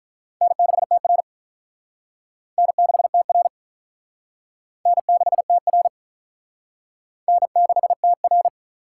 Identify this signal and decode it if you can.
Morse code, heard as a keyed tone, about 45 wpm, 700 Hz. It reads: N6TR N6TR N6TR N6TR